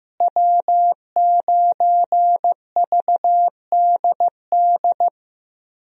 Morse code, heard as a keyed tone, about 15 wpm, 700 Hz. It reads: W9VDD
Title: USA Call Signs - 5 Characters - Repeated 2x - 15wpm